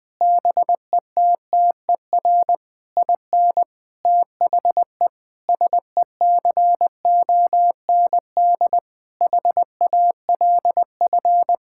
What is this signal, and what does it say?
Morse code, heard as a keyed tone, about 20 wpm, 700 Hz. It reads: BETTER IN THE SECOND HALF
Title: Sentences with 5-Word Phrases - Spoken Beginning - 20wpm